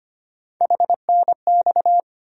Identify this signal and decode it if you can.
Morse code, heard as a keyed tone, about 25 wpm, 700 Hz. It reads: HNX
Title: USA Call Sign Postfixes - 3 Letters 25wpm